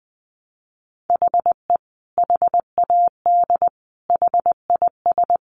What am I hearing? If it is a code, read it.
Morse code, heard as a keyed tone, about 20 wpm, 700 Hz. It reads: HE HAD HIS
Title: Sentences from Top 1200 Words 20wpm